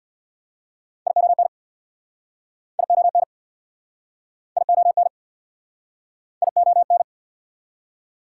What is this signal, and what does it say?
Morse code, heard as a keyed tone, about 69 wpm, 690 Hz. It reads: ION ION ION ION